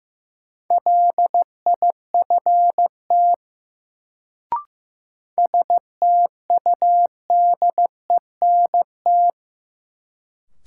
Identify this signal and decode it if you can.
Morse code, heard as a keyed tone, about 15 wpm, 700 Hz. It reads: LIFT STUDENT
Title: Sentences from Top 1000 Words - New Word Drill - 15wpm